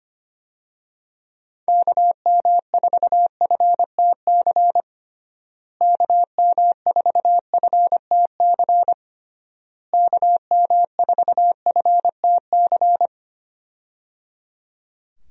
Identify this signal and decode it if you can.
Morse code, heard as a keyed tone, about 25 wpm, 700 Hz. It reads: KM4FTC KM4FTC KM4FTC
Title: USA Call Signs - 6 Characters - Repeated 3x - 25wpm